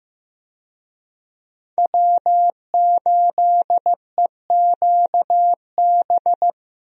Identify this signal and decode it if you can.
Morse code, heard as a keyed tone, about 15 wpm, 700 Hz. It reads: W8EQB
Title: USA Call Signs - 5 Characters - Repeated 2x - 15wpm